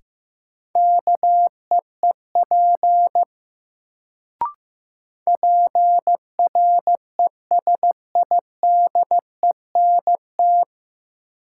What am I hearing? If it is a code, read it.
Morse code, heard as a keyed tone, about 15 wpm, 700 Hz. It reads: KEEP PRESIDENT
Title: Sets of 2 Words - New Word Drill - 15wpm